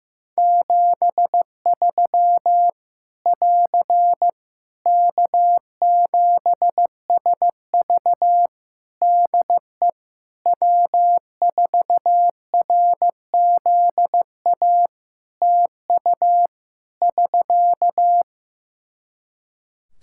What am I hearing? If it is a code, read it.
Morse code, heard as a keyed tone, about 15 wpm, 700 Hz. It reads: 73 <AR> K7SV DE W4RZA TU <SK>
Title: Practice QSO - Exchange 4 - Ending - 15wpm